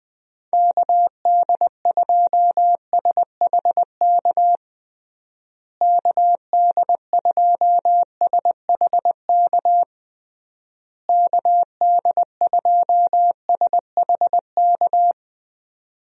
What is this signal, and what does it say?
Morse code, heard as a keyed tone, about 20 wpm, 700 Hz. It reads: KD2SHK KD2SHK KD2SHK
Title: USA Call Signs - 6 Characters - Repeated 3x - 20wpm